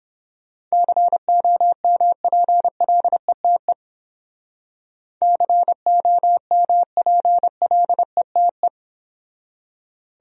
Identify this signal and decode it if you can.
Morse code, heard as a keyed tone, about 32 wpm, 700 Hz. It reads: COMPLETE COMPLETE